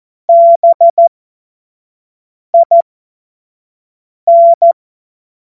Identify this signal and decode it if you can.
Morse code, heard as a keyed tone, about 14 wpm, spaced as for 5 wpm, 680 Hz. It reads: BIN
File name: Morsekurs / Witze / Witz 07. März 2023 25 F70